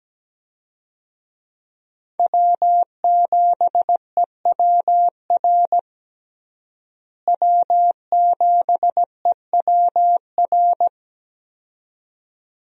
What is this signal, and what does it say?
Morse code, heard as a keyed tone, about 17 wpm, 700 Hz. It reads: W7EWR W7EWR
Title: USA Call Signs - 5 Characters - Repeated 2x - 17wpm